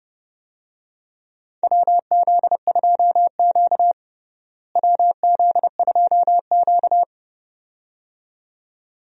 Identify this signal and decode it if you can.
Morse code, heard as a keyed tone, about 32 wpm, 700 Hz. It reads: WZ2Q WZ2Q
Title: USA Call Signs - 4 Characters - Repeated 2x - 30wpm